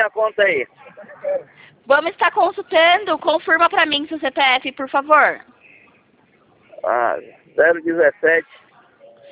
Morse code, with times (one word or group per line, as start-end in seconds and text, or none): none